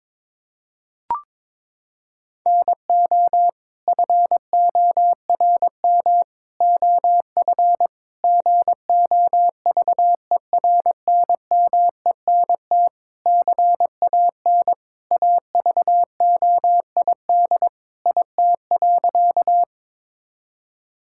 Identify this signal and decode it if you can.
Morse code, heard as a keyed tone, about 22 wpm, 700 Hz. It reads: NO FORM OF GOVERNMENT CAN AVOID IT.
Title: Sentences from Top 1000 Words - Review Part 6 - 22wpm